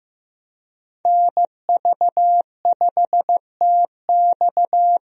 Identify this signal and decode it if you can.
Morse code, heard as a keyed tone, about 15 wpm, 700 Hz. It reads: NV5TX
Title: USA Call Signs - 5 Characters - Repeated 2x - 15wpm